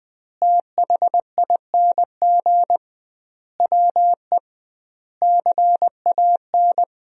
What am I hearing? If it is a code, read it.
Morse code, heard as a keyed tone, about 20 wpm, 700 Hz. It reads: THING WE CAN
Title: 2x Word-Spacing - Sentences from Top 200 Words - 20wpm